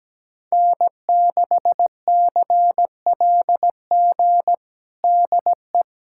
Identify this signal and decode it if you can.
Morse code, heard as a keyed tone, about 17 wpm, 700 Hz. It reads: N6CLG DE
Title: USA Callsign DE Callsign - 5 Characters - 17wpm